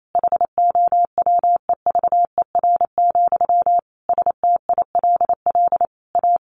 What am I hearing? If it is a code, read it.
Morse code, heard as a keyed tone, about 28 wpm, 700 Hz. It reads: HOWEVER, STILL A
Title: Sentences from Top 1400 Words 28wpm